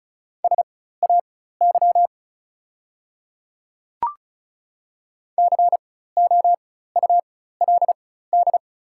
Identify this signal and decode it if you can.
Morse code, heard as a keyed tone, about 36 wpm, spaced as for 16 wpm, 690 Hz. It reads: SAY COULD
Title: 4x Character Spacing - Top 300 Words - 35wpm